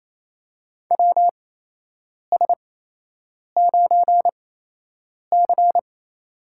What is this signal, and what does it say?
Morse code, heard as a keyed tone, about 28 wpm, spaced as for 8 wpm, 700 Hz. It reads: WS9C
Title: USA Call Signs - 8x Character Spacing - 28wpm